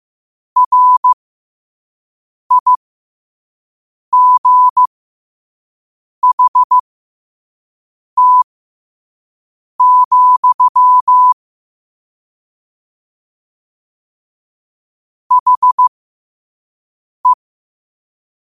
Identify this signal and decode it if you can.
Morse code, heard as a keyed tone, about 15 wpm, spaced as for 5 wpm, 1000 Hz. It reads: RIGHT, HE
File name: qotdfiles / 5wpm